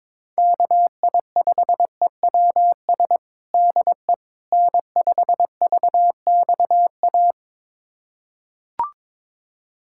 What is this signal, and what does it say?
Morse code, heard as a keyed tone, about 22 wpm, 700 Hz. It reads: KI5EWS DE N5VXA
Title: USA Callsign DE Callsign 22wpm